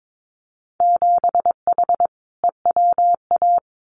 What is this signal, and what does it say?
Morse code, heard as a keyed tone, about 22 wpm, 700 Hz. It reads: 7H EWA
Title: Winter Field Day Exchanges Encore 22wpm